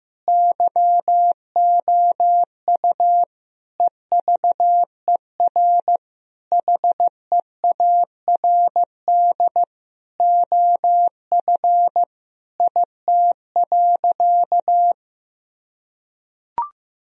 Morse code, heard as a keyed tone, about 15 wpm, 700 Hz. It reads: YOU EVER HEARD OF IT.
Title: Sentences from Top 1000 Words - Review Part 3 - 15wpm